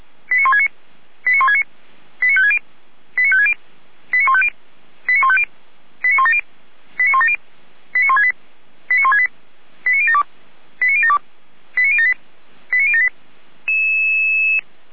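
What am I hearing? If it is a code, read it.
Five-tone selective call (ZVEI), heard as quick runs of five tones, tones of about 70 ms. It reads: 89169 89169 89560 89560 89150 89150 89170 89170 89168 89168 89072 89072 89078 89078
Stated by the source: ZVEI five-tone selective call received by SDR, fire brigade Feuerwehr Orlamünde.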